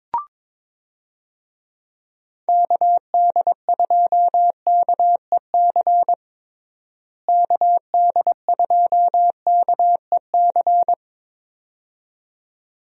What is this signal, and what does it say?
Morse code, heard as a keyed tone, about 21 wpm, 700 Hz. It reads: KD2KEC KD2KEC
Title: USA Call Signs - 6 Characters - Repeated 2x - 22wpm